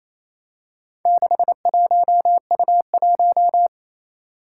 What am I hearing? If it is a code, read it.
Morse code, heard as a keyed tone, about 27 wpm, 700 Hz. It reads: 61U1
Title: Four Challenging Letter-Number Combinations 28wpm